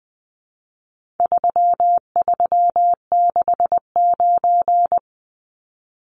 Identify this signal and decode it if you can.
Morse code, heard as a keyed tone, about 20 wpm, 700 Hz. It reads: 3369
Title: Numbers - 4-Digit 20wpm